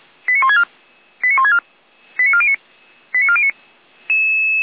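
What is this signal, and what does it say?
Five-tone selective call (ZVEI), heard as quick runs of five tones, tones of about 70 ms. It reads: 89164 89164 89409 89409
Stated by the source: ZVEI five-tone selective call received on an SDR.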